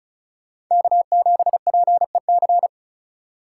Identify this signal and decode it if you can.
Morse code, heard as a keyed tone, about 45 wpm, 700 Hz. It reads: K7PEC